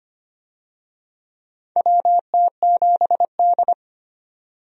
Morse code, heard as a keyed tone, about 25 wpm, 700 Hz. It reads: WT7D
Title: USA Call Signs - 4 Characters - 25wpm